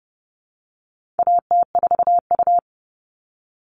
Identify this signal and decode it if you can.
Morse code, heard as a keyed tone, about 30 wpm, 700 Hz. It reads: AT4U
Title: All Cut Numbers - 4-Digit 30wpm